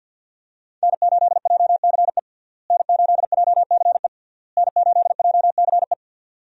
Morse code, heard as a keyed tone, about 51 wpm, 700 Hz. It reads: N8JCE N8JCE N8JCE
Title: USA Call Signs - 5 Characters - Repeated 3x - 50wpm